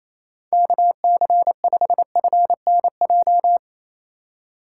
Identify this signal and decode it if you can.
Morse code, heard as a keyed tone, about 27 wpm, 700 Hz. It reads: KC5FNJ
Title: USA Call Signs - 6 Characters - 4x Character Spacing - 28wpm